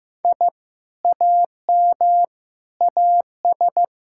Morse code, heard as a keyed tone, about 15 wpm, 700 Hz. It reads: I AM AS
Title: Sentences from Top 1000 Words - Review Part 6 - 15wpm